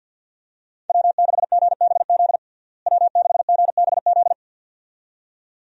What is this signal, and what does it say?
Morse code, heard as a keyed tone, about 52 wpm, 700 Hz. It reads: W6GBZ W6GBZ
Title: USA Call Signs - 5 Characters - Repeated 2x - 50wpm